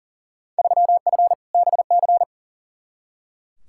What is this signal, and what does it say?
Morse code, heard as a keyed tone, about 40 wpm, 700 Hz. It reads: 3F BC
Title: Field Day Exchanges Encore - 40wpm